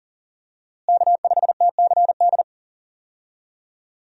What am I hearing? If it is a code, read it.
Morse code, heard as a keyed tone, about 40 wpm, 700 Hz. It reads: K5TCD
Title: USA Call Signs - 5 Characters - Repeated 2x - 40wpm